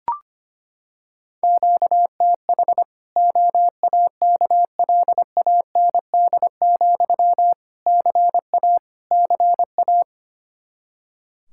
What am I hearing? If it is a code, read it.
Morse code, heard as a keyed tone, about 25 wpm, 700 Hz. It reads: QTH OAKLAND, CA CA
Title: Practice QSO - Exchange 1 - Beginning - 25wpm